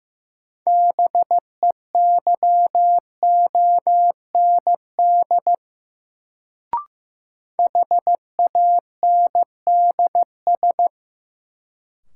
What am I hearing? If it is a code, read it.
Morse code, heard as a keyed tone, about 15 wpm, 700 Hz. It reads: BEYOND HANDS
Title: Sentences from Top 1000 Words - Review Part 6 - New Word Drill - 15wpm